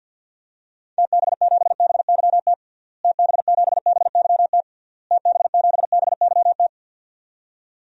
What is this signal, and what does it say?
Morse code, heard as a keyed tone, about 49 wpm, 700 Hz. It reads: TB7BYT TB7BYT TB7BYT
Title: DXCC-100 Call Signs - 6 Characters - Repeated 3x - 50wpm